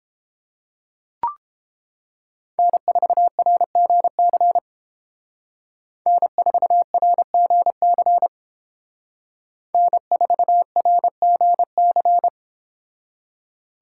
Morse code, heard as a keyed tone, about 29 wpm, 700 Hz. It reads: N4RGC N4RGC N4RGC